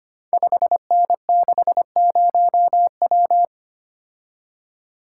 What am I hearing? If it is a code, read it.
Morse code, heard as a keyed tone, about 25 wpm, 700 Hz. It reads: 5N60W
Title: Code Groups - 5 Letter:Numbers - 25wpm